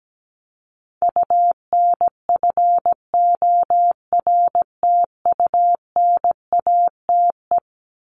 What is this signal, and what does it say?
Morse code, heard as a keyed tone, about 17 wpm, 700 Hz. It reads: UNFORTUNATE
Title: LY Suffix - Top 100 Words with Base Words 17wpm